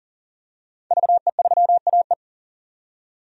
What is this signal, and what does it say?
Morse code, heard as a keyed tone, about 40 wpm, 700 Hz. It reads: VE3AE
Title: DXCC-100 Call Signs - 5 Characters - 40wpm